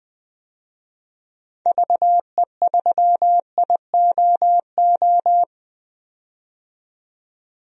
Morse code, heard as a keyed tone, about 21 wpm, 700 Hz. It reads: VE3IOO
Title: DXCC-100 Call Signs - 6 Characters - 20wpm